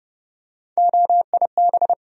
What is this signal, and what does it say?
Morse code, heard as a keyed tone, about 31 wpm, 700 Hz. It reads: OIB
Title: USA Call Sign Postfixes - 3 Letters 30wpm